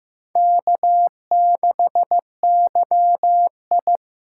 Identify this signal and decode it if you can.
Morse code, heard as a keyed tone, about 15 wpm, 700 Hz. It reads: K6YI
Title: USA Call Signs - 4 Characters - 3x Character Spacing - 15wpm